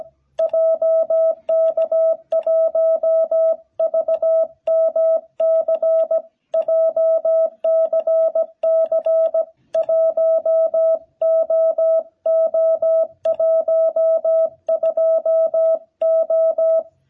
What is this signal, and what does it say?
Morse code, heard as a keyed tone, about 17 wpm, 660 Hz. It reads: JK1VMC JCC1OO12O
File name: jh8cbh / 2021cw